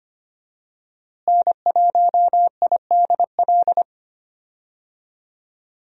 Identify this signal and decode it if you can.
Morse code, heard as a keyed tone, about 25 wpm, 700 Hz. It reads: N1IDL
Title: USA Call Signs - 5 Characters - Repeated 2x - 25wpm